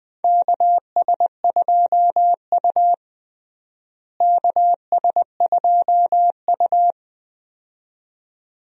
Morse code, heard as a keyed tone, about 20 wpm, 700 Hz. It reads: KS2U KS2U
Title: USA Call Signs - 4 Characters - Repeated 2x - 20wpm